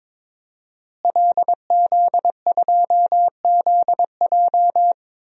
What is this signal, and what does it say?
Morse code, heard as a keyed tone, about 22 wpm, 700 Hz. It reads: LZ2ZJ
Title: DXCC-100 Call Signs - 5 Characters - Repeated 2x - 22wpm